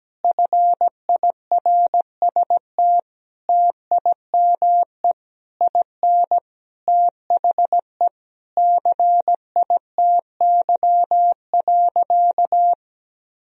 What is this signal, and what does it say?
Morse code, heard as a keyed tone, about 17 wpm, 700 Hz. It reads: FIRST TIME IN THE CITY.
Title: Sentences from Top 1000 Words - Review Part 4 - 17wpm